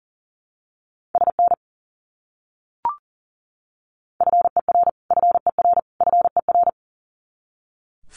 Morse code, heard as a keyed tone, about 40 wpm, 700 Hz. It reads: SN FER FER FER
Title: Abbreviations Repeated 3x - 40wpm